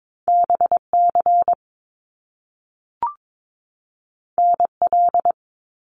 Morse code, heard as a keyed tone, about 22 wpm, 700 Hz. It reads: BC NL
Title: Canadian Provinces and Territories 22wpm